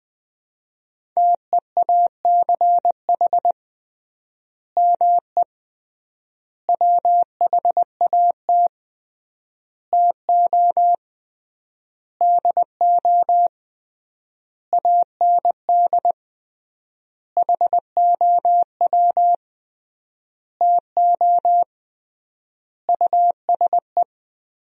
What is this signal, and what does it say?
Morse code, heard as a keyed tone, about 20 wpm, 700 Hz. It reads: TEACH ME WHAT TO DO AND HOW TO USE
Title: 3x Word-Spacing - Sentences from Top 500 Words - 20wpm